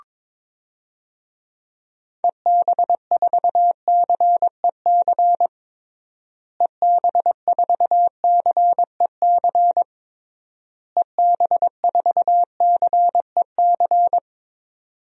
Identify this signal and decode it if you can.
Morse code, heard as a keyed tone, about 22 wpm, 700 Hz. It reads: EB4CEC EB4CEC EB4CEC
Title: DXCC-100 Call Signs - 6 Characters - Repeated 3x - 22wpm